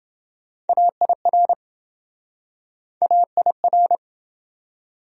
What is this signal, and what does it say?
Morse code, heard as a keyed tone, about 29 wpm, 700 Hz. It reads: AIR AIR